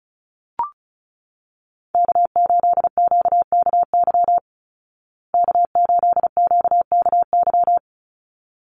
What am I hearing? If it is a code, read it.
Morse code, heard as a keyed tone, about 34 wpm, 700 Hz. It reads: K8QKY K8QKY
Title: Call Signs Repeated 3x - 35wpm